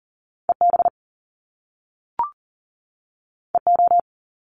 Morse code, heard as a keyed tone, about 41 wpm, 700 Hz. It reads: EB EO
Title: Top Two Letter Combinations in Words 40wpm